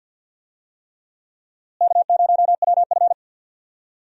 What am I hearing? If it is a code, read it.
Morse code, heard as a keyed tone, about 49 wpm, 700 Hz. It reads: K0WF
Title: USA Call Signs - 4 Characters - Repeated 3x - 50wpm